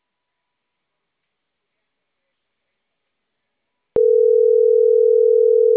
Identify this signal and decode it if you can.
Ringback tone (North America), 440 + 480 Hz.